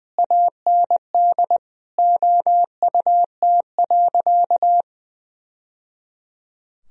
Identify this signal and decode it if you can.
Morse code, heard as a keyed tone, about 21 wpm, 700 Hz. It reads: AND OUT.
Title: Sentences from Top 1000 Words - Review Part 2 - 20wpm